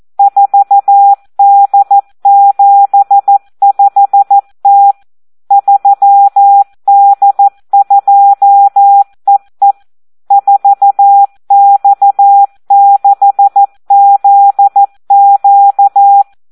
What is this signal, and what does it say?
Morse code, heard as a keyed tone, about 14 wpm, 800 Hz. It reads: 4D75T 3D2EE 4X6ZQ
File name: Audio practica CW / SD 1 14